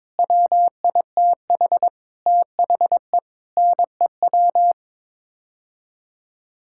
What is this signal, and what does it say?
Morse code, heard as a keyed tone, about 22 wpm, 700 Hz. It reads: WITH THE NEW
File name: Sets of 9 Words - 22wpm